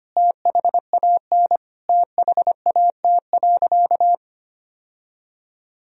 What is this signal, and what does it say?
Morse code, heard as a keyed tone, about 25 wpm, 700 Hz. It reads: THAN THAT.
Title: Sentences from Top 1000 Words - Review Part 3 - 25wpm